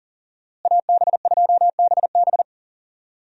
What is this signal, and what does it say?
Morse code, heard as a keyed tone, about 37 wpm, 700 Hz. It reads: AB2BB
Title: USA Call Signs - 5 Characters - Repeated 3x - 40wpm